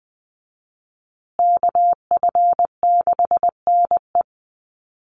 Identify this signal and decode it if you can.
Morse code, heard as a keyed tone, about 20 wpm, 700 Hz. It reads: KF6NE
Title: USA Call Signs Repeated 3x 20wpm